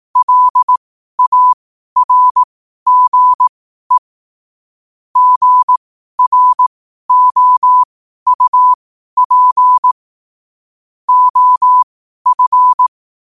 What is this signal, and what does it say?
Morse code, heard as a keyed tone, about 18 wpm, spaced as for 12 wpm, 1000 Hz. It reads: LARGE GROUP OF
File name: qotdfiles / 12wpm